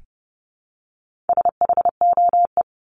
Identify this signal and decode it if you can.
Morse code, heard as a keyed tone, about 30 wpm, 700 Hz. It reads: SHOE
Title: Top 1751-2000 Words 30wpm